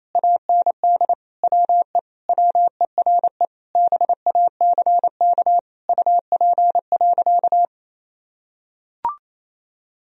Morse code, heard as a keyed tone, about 28 wpm, 700 Hz. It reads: AND WE WERE BACK UP.
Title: Sentences from Top 1000 Words - Review Part 2 - 28wpm